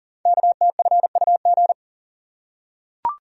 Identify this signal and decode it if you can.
Morse code, heard as a keyed tone, about 40 wpm, 700 Hz. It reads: KTFUG
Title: Code Groups - 5 Letters - 40wpm